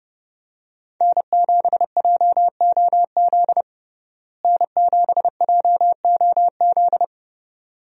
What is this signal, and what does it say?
Morse code, heard as a keyed tone, about 30 wpm, 700 Hz. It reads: N7JOZ N7JOZ
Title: USA Call Signs - 5 Characters - Repeated 2x - 30wpm